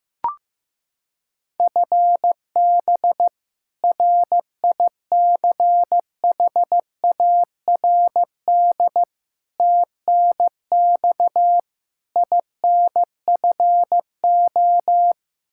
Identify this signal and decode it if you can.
Morse code, heard as a keyed tone, about 15 wpm, 700 Hz. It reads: FB RICHARD TNX INFO
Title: Practice QSO - Exchange 3 - More Info - 15wpm